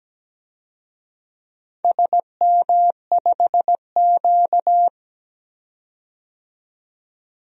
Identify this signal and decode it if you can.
Morse code, heard as a keyed tone, about 17 wpm, 700 Hz. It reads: SM5Q
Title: DXCC-100 Call Signs - 4 Characters - 17wpm